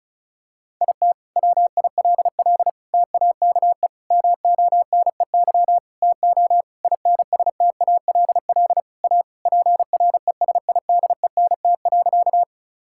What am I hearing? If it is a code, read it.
Morse code, heard as a keyed tone, about 34 wpm, 700 Hz. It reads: IT WILL TAKE MONEY TO INSTALL A PRESIDENT.
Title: Sentences from Top 1600 Words 35wpm